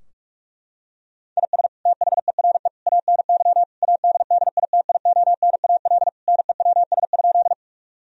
Morse code, heard as a keyed tone, about 46 wpm, 700 Hz. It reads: IS THERE ANY ADDITIONAL NEWS?
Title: Sentences from Top 1000 Words - Review Part 4 - 45wpm